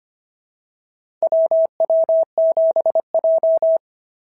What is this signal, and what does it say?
Morse code, heard as a keyed tone, about 25 wpm, 690 Hz. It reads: WW7J
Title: SOTA QSO Practice 25wpm